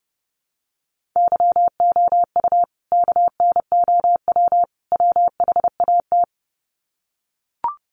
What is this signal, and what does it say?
Morse code, heard as a keyed tone, about 30 wpm, 700 Hz. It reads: YOU KNOW WHAT
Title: Sets of 3 Words 30wpm